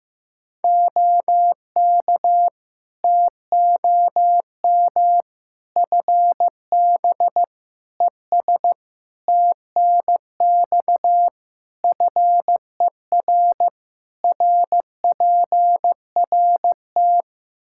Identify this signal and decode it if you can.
Morse code, heard as a keyed tone, about 15 wpm, 700 Hz. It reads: OK TOM FB ES TNX FER RPRT
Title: Practice QSO - Exchange 2 - Info - 15wpm